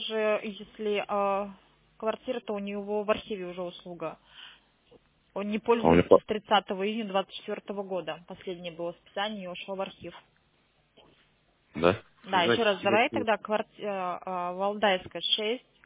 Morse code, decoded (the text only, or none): none